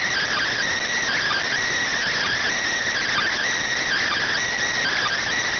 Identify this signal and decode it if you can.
Emergency Alert System (SAME) tones (header bursts), heard as harsh buzzing bursts. It is heard from 1.0 to 5.2 s.